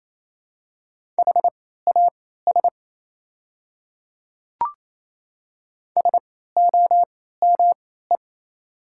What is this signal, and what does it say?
Morse code, heard as a keyed tone, about 28 wpm, spaced as for 16 wpm, 690 Hz. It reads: HAS SOME
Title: 3x Character Spacing - Top 200 Words - 28wpm